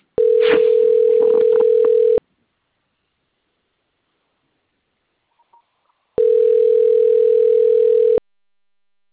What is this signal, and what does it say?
Ringback tone (North America), 440 + 480 Hz.